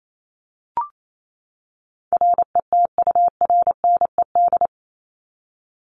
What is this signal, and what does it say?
Morse code, heard as a keyed tone, about 28 wpm, 700 Hz. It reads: RETURNED
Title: ED Suffix - Top 100 Words with Base Words 28wpm